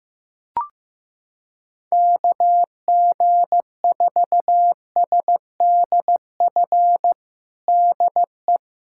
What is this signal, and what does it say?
Morse code, heard as a keyed tone, about 15 wpm, 700 Hz. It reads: KG4SDF DE
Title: USA Callsign DE Callsign - 6 Characters - 15wpm